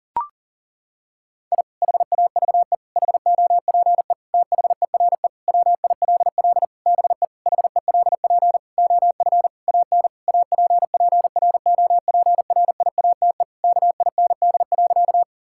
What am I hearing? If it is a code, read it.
Morse code, heard as a keyed tone, about 37 wpm, 700 Hz. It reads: I HAVE HOPE THERE WILL BE HELP OF AN APPROPRIATE KIND.
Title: Sentences from Top 1000 Words - Review Part 5 - 40wpm